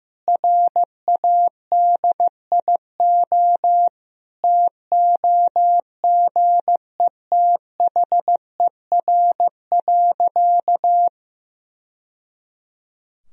Morse code, heard as a keyed tone, about 15 wpm, 700 Hz. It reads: RADIO TOGETHER.
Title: Sentences from Top 1000 Words - Review Part 3 - 15wpm